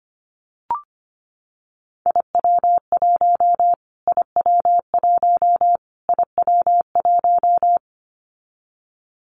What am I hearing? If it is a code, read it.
Morse code, heard as a keyed tone, about 25 wpm, 700 Hz. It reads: IW1 IW1 IW1
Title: Callsign Prefixes for Top 25 DXCC - Repeated 3x - 25wpm